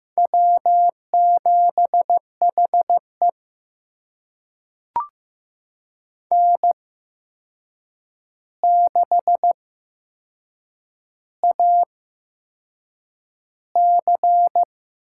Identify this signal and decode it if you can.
Morse code, heard as a keyed tone, about 15 wpm, spaced as for 6 wpm, 700 Hz. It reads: W7HE N6AC
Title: USA Call Signs - 4 Characters - 8x Character Spacing - 15wpm